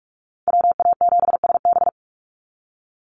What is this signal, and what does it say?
Morse code, heard as a keyed tone, about 44 wpm, 690 Hz. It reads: WA7SB
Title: NA SOTA Chaser Callsigns 45wpm